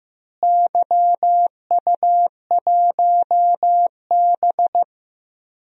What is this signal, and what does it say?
Morse code, heard as a keyed tone, about 16 wpm, 700 Hz. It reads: YU1B
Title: DXCC-100 Call Signs - 4 Characters - Repeated 3x - 15wpm